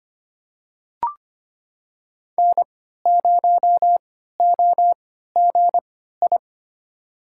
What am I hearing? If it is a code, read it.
Morse code, heard as a keyed tone, about 25 wpm, spaced as for 14 wpm, 700 Hz. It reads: N0OGI
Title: USA Call Signs - 5 Characters - 3x Character Spacing - 25wpm